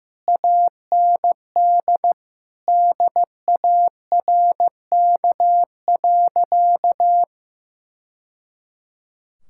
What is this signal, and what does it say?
Morse code, heard as a keyed tone, about 15 wpm, 700 Hz. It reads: AND DARK.
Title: Sentences from Top 1000 Words - Review Part 5 - 15wpm